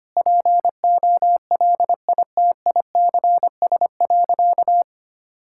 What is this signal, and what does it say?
Morse code, heard as a keyed tone, about 25 wpm, 700 Hz. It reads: POLITICS.
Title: Sentences from Top 1000 Words - Review Part 6 - 25wpm